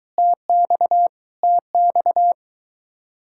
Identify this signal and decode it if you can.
Morse code, heard as a keyed tone, about 23 wpm, 700 Hz. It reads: TX TX